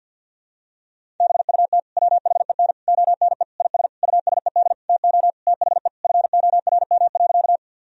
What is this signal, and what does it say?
Morse code, heard as a keyed tone, about 49 wpm, 700 Hz. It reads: BUT WHEN ONE IS USED TO THE FORM.
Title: Sentences from Top 1000 Words - Review Part 1 - 50wpm